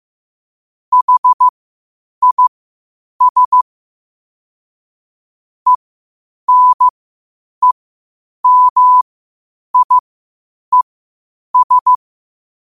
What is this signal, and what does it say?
Morse code, heard as a keyed tone, about 15 wpm, spaced as for 8 wpm, 1000 Hz. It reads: HIS ENEMIES
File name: qotdfiles / 8wpm